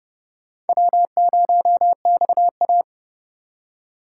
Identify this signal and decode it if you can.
Morse code, heard as a keyed tone, about 39 wpm, 700 Hz. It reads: W0XA